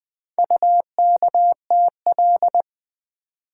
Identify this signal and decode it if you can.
Morse code, heard as a keyed tone, about 20 wpm, 700 Hz. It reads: UKTL
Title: Four Letter-Number Combinations - 20wpm